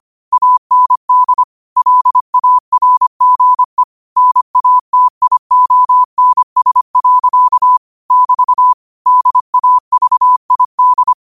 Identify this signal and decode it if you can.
Morse code, heard as a keyed tone, about 25 wpm, 1000 Hz. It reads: AND LARGE NATIONS. <BT> DAVID
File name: qotdfiles / 25wpm